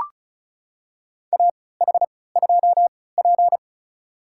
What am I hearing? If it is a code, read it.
Morse code, heard as a keyed tone, about 34 wpm, spaced as for 20 wpm, 700 Hz. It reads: AH2P
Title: USA Call Signs - 3x Character Spacing - 35wpm